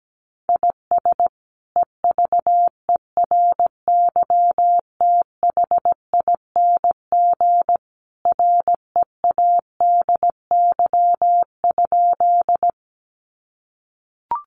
Morse code, heard as a keyed tone, about 17 wpm, 700 Hz. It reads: IS EVERYTHING READY?
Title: Sentences from Top 700 Words 17wpm